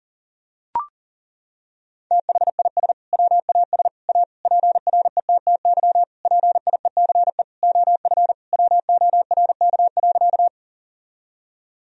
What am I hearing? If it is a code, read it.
Morse code, heard as a keyed tone, about 40 wpm, 700 Hz. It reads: THIS WAS A PRETTY PIECE OF WORK.
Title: Sentences from Top 1000 Words - Review Part 2 - 40wpm